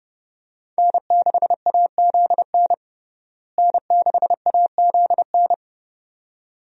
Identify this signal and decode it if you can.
Morse code, heard as a keyed tone, about 30 wpm, 700 Hz. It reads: N6AZN N6AZN
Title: USA Call Signs - 5 Characters - Repeated 2x - 30wpm